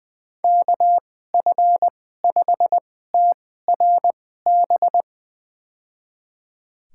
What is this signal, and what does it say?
Morse code, heard as a keyed tone, about 20 wpm, spaced as for 14 wpm, 700 Hz. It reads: KF5TRB
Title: USA Call Signs - 2x Character Spacing - 20wpm